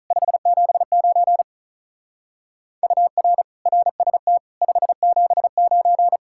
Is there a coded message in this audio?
Yes: Morse code from 0.1 to 6.2 s, heard as a keyed tone, about 46 wpm, 700 Hz. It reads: 579 UR RST 579